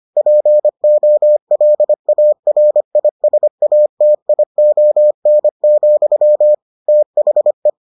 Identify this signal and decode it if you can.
Morse code, heard as a keyed tone, about 25 wpm, 600 Hz. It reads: POLARISATION, THE